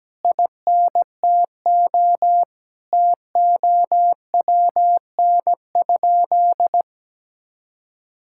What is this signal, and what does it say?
Morse code, heard as a keyed tone, about 17 wpm, 700 Hz. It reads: INTO TOWN?
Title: Sentences from Top 1000 Words - Review Part 5 - 17wpm